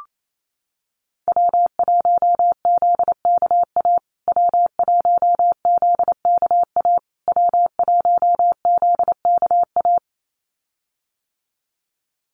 Morse code, heard as a keyed tone, about 28 wpm, 700 Hz. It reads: W1ZKA W1ZKA W1ZKA
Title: USA Call Signs Repeated 3x 28wpm